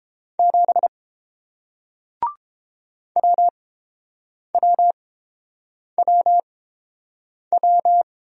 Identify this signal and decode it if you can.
Morse code, heard as a keyed tone, about 30 wpm, spaced as for 7 wpm, 700 Hz. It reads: 7 WWWW